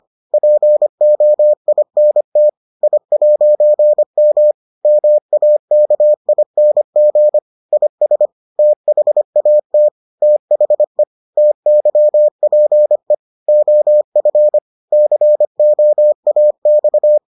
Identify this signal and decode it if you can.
Morse code, heard as a keyed tone, about 25 wpm, 600 Hz. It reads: POINT I'M MAKING IS THAT THE TYPE OF COAX